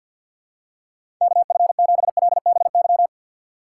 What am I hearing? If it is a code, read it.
Morse code, heard as a keyed tone, about 52 wpm, 700 Hz. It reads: KF7LBY
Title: USA Call Signs - 6 Characters - 3x Character Spacing - 50wpm